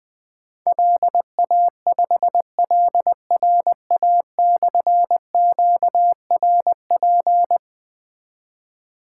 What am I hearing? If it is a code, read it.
Morse code, heard as a keyed tone, about 20 wpm, 700 Hz. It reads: LA5LRA/QRP
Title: Difficult Call Signs 20wpm